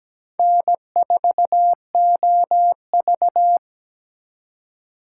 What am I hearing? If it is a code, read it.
Morse code, heard as a keyed tone, about 20 wpm, 700 Hz. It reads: N4OV
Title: Speed Racing - USA Call Signs 17wpm